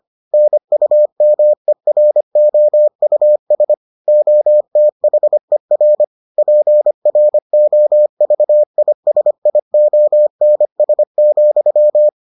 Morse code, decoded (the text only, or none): NUMEROUS OTHER PROVISIONS,